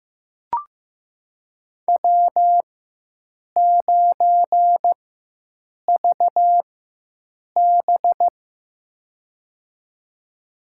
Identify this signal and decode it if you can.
Morse code, heard as a keyed tone, about 15 wpm, spaced as for 7 wpm, 700 Hz. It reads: W9VB
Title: USA Call Signs - 4 Characters - 4x Character Spacing - 15wpm